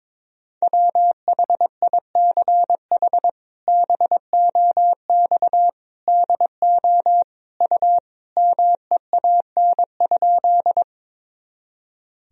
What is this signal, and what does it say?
Morse code, heard as a keyed tone, about 23 wpm, 700 Hz. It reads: WHICH BOX DO U MEAN?
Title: Sentences from Top 500 Words and Abbreviations 22wpm